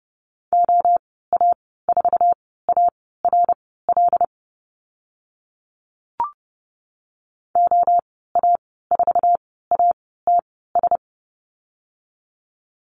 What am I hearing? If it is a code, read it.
Morse code, heard as a keyed tone, about 30 wpm, spaced as for 17 wpm, 700 Hz. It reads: OA4ARL OA4ATS
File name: Lesson 14g - Number 2 - Call Signs 30wpm